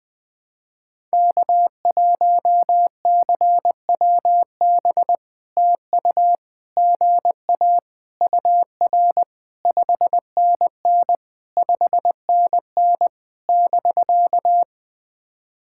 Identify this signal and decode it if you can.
Morse code, heard as a keyed tone, about 18 wpm, 700 Hz. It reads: K1CWB TU GA UR 5NN 5NN <BK>
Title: POTA QSO Practice - 20wpm